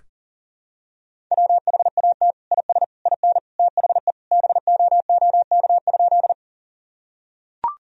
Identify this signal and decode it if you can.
Morse code, heard as a keyed tone, about 37 wpm, 700 Hz. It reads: WHAT IS IN THE BOOK?
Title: Sentences from Top 1000 Words - Review Part 1 - 40wpm